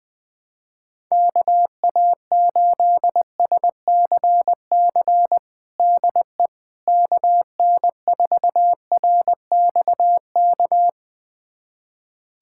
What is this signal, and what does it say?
Morse code, heard as a keyed tone, about 20 wpm, 700 Hz. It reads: KA8SCC DE KN4RXK
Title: USA Callsign DE Callsign 20wpm